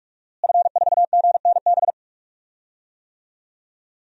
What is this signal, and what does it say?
Morse code, heard as a keyed tone, about 46 wpm, 700 Hz. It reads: F4GNB
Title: DXCC-100 Call Signs - 5 Characters - Repeated 2x - 45wpm